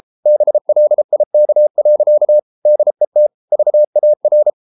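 Morse code, heard as a keyed tone, about 33 wpm, 600 Hz. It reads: BLIK. DET VAR
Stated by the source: Morse code at 33 wpm, QRQ.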